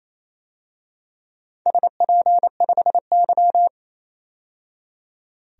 Morse code, heard as a keyed tone, about 27 wpm, 700 Hz. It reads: SP5Y
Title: DXCC-100 Call Signs - 4 Characters - 28wpm